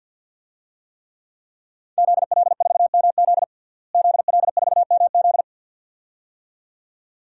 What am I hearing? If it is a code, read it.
Morse code, heard as a keyed tone, about 51 wpm, 700 Hz. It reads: ZL4MZ ZL4MZ
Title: DXCC-100 Call Signs - 5 Characters - Repeated 2x - 50wpm